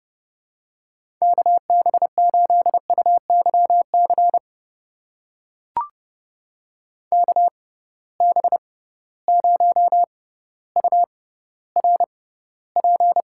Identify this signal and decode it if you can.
Morse code, heard as a keyed tone, about 31 wpm, spaced as for 15 wpm, 700 Hz. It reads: KB8UYC KB0URP
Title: USA Call Signs - 6 Characters - 6x Character Spacing - 30wpm